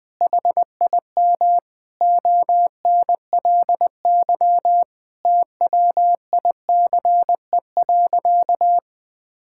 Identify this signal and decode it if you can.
Morse code, heard as a keyed tone, about 20 wpm, 700 Hz. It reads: HIM ONLY TWICE.
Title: Sentences from Top 1600 Words 20wpm